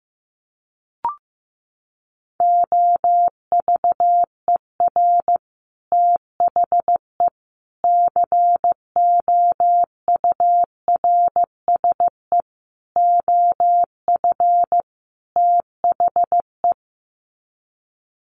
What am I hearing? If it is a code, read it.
Morse code, heard as a keyed tone, about 15 wpm, 700 Hz. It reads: OVER THE COURSE OF THE
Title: Sets of 5 Words 15wpm